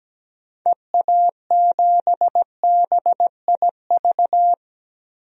Morse code, heard as a keyed tone, about 17 wpm, 700 Hz. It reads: EA7BIV
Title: DXCC-100 Call Signs - 6 Characters - 17wpm